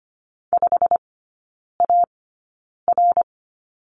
Morse code, heard as a keyed tone, about 25 wpm, spaced as for 9 wpm, 690 Hz. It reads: 5AR
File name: Lesson 15e - Number 5 - Three Characters 25wpm